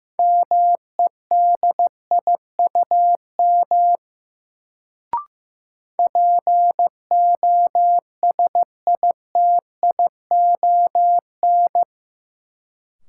Morse code, heard as a keyed tone, about 15 wpm, 700 Hz. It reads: MEDIUM POSITION
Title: Sentences from Top 900 Words - New Word Drill - 15wpm